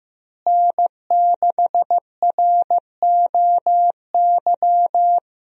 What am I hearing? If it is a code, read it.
Morse code, heard as a keyed tone, about 16 wpm, 700 Hz. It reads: N6ROY
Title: USA Call Signs - 5 Characters - Repeated 3x - 15wpm